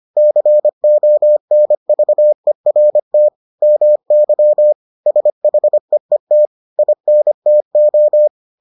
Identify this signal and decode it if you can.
Morse code, heard as a keyed tone, about 25 wpm, 600 Hz. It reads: CONVERT MY SHEET INTO